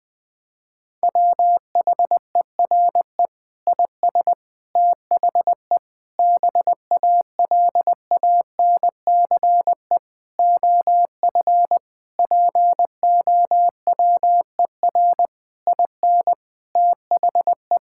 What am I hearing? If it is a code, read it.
Morse code, heard as a keyed tone, about 21 wpm, 700 Hz. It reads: WHERE IS THE BALANCE OF POWER IN THE
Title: Sentences from Top 1000 Words - Review Part 6 - 20wpm